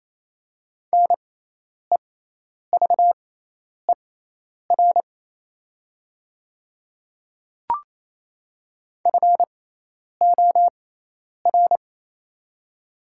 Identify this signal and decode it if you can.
Morse code, heard as a keyed tone, about 28 wpm, spaced as for 10 wpm, 700 Hz. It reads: NEVER FOR
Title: 6x Character Spacing - Top 300 Words - 28wpm